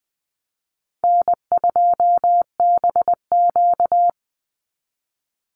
Morse code, heard as a keyed tone, about 20 wpm, 700 Hz. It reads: N2BQ
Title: USA Call Signs 20wpm